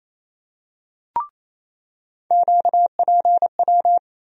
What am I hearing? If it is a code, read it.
Morse code, heard as a keyed tone, about 28 wpm, 700 Hz. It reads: QPW
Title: USA Call Sign Postfixes - 3 Letters 28wpm